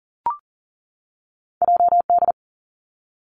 Morse code, heard as a keyed tone, about 40 wpm, 700 Hz. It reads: JD
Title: Two Challenging Letter-Number Combinations 40wpm